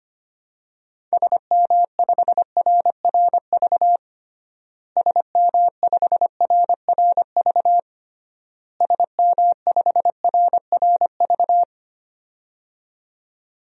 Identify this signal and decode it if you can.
Morse code, heard as a keyed tone, about 25 wpm, 700 Hz. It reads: SM5RRV SM5RRV SM5RRV
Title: DXCC-100 Call Signs - 6 Characters - Repeated 3x - 25wpm